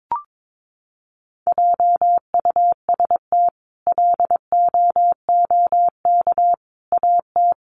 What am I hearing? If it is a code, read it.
Morse code, heard as a keyed tone, about 22 wpm, 700 Hz. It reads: JUST LOOK AT
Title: Sentences from Top 1000 Words 22wpm